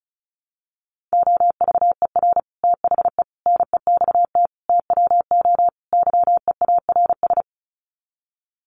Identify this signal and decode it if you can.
Morse code, heard as a keyed tone, about 34 wpm, 700 Hz. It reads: OVER THE NEXT TWO YEARS
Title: Sets of 5 Words Encore2 35wpm